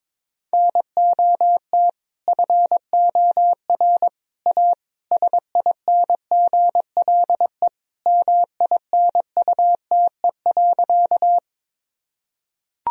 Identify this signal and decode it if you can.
Morse code, heard as a keyed tone, about 21 wpm, 700 Hz. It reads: NOT FOR A SINGLE MINUTE.
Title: Sentences from Top 1000 Words - Review Part 4 - 22wpm